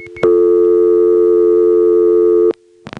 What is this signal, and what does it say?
Dial tone (North America), 350 + 440 Hz.
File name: phonetrips / SmyrnaGA5HQ